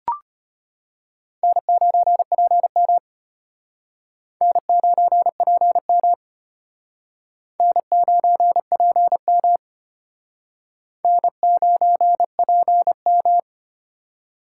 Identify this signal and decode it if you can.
Morse code, heard as a keyed tone, about 31 wpm, 700 Hz. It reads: N9PM N9PM N9PM N9PM
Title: Speed Racing - USA 1x2 Call Signs 25wpm